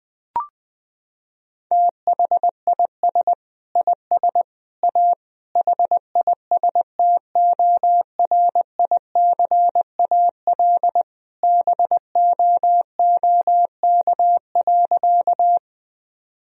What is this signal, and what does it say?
Morse code, heard as a keyed tone, about 19 wpm, 700 Hz. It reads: THIS IS A HISTORICAL BOOK.
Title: Sentences from Top 1000 Words - Review Part 4 - 20wpm